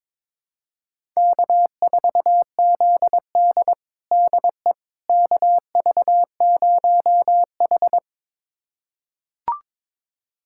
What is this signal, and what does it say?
Morse code, heard as a keyed tone, about 22 wpm, 700 Hz. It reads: K4ZD DE KV0H
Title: USA Callsign DE Callsign - 4 Characters - 22wpm